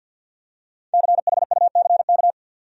Morse code, heard as a keyed tone, about 51 wpm, 700 Hz. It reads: CHUCK
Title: USA Common Names - Max 5 Letters - 50wpm